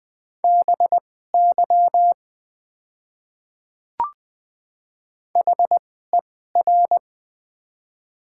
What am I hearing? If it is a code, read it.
Morse code, heard as a keyed tone, about 20 wpm, spaced as for 14 wpm, 700 Hz. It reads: BY HER
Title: 2x Character Spacing - Top 100 Words - 20wpm